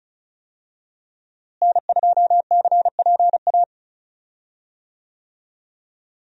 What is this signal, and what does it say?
Morse code, heard as a keyed tone, about 34 wpm, 700 Hz. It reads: N2CPA
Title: USA Call Signs - 5 Characters - Repeated 3x - 35wpm